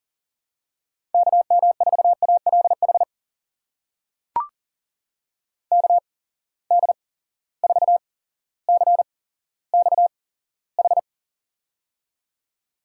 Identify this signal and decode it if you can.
Morse code, heard as a keyed tone, about 40 wpm, spaced as for 17 wpm, 700 Hz. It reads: KM4ALH KD4CXH
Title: USA Call Signs - 6 Characters - 8x Character Spacing - 40wpm